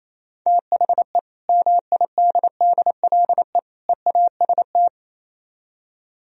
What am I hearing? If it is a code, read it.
Morse code, heard as a keyed tone, about 33 wpm, 700 Hz. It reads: THE MIDDLE EAST
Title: Speed-Racing - Sets of 3 Words 28wpm